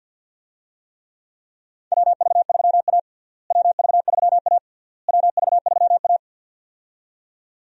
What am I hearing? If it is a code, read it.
Morse code, heard as a keyed tone, about 49 wpm, 700 Hz. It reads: WV3A WV3A WV3A
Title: USA Call Signs - 4 Characters - Repeated 3x - 50wpm